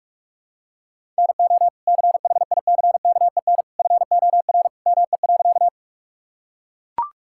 Morse code, heard as a keyed tone, about 41 wpm, 700 Hz. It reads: NO CHICKEN FOR ME.
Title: Sentences from Top 1000 Words - Review Part 5 - 45wpm